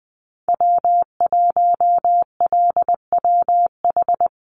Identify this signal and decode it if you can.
Morse code, heard as a keyed tone, about 20 wpm, 700 Hz. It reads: W1LWH
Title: Call Signs Repeated 2x - 20wpm